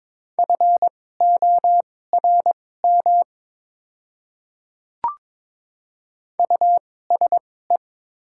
Morse code, heard as a keyed tone, about 22 wpm, spaced as for 16 wpm, 690 Hz. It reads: FORM USE
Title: 2x Character Spacing - Top 200 Words - 22wpm